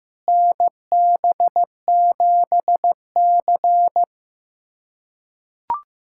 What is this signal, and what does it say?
Morse code, heard as a keyed tone, about 16 wpm, 700 Hz. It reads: NB7C
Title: USA Callsign DE Callsign - 4 Characters - 15wpm